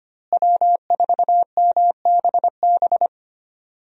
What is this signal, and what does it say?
Morse code, heard as a keyed tone, about 25 wpm, 700 Hz. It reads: W4MBB
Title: USA Call Signs - 5 Characters - Repeated 2x - 25wpm